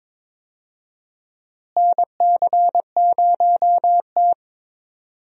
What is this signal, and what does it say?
Morse code, heard as a keyed tone, about 22 wpm, 700 Hz. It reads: NC0T
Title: USA Call Signs - 4 Characters - Repeated 3x - 22wpm